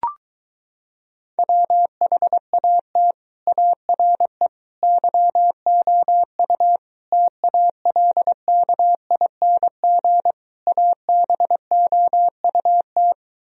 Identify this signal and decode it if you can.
Morse code, heard as a keyed tone, about 23 wpm, 700 Hz. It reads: WHAT ARE YOU TALKING ABOUT